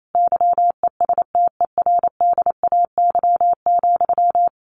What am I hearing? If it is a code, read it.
Morse code, heard as a keyed tone, about 28 wpm, 700 Hz. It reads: YESTERDAY,
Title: A Sound Of Thunder 28wpm